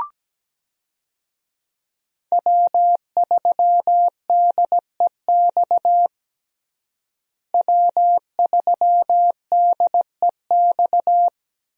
Morse code, heard as a keyed tone, about 17 wpm, 700 Hz. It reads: W3DEX W3DEX
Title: USA Call Signs - 5 Characters - Repeated 2x - 17wpm